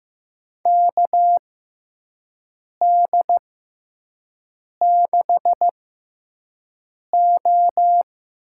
Morse code, heard as a keyed tone, about 15 wpm, spaced as for 5 wpm, 700 Hz. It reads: KD6O
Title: USA Call Signs - 6 Characters - 6x Character Spacing - 15wpm